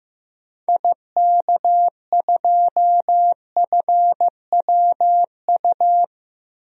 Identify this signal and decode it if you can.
Morse code, heard as a keyed tone, about 15 wpm, 700 Hz. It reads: IK2FWU
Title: DXCC-100 Call Signs - 6 Characters - 15wpm